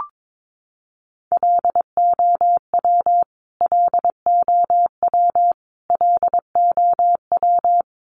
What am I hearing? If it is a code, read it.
Morse code, heard as a keyed tone, about 22 wpm, 700 Hz. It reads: LOW LOW LOW
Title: Top 200 Words Repeated 22wpm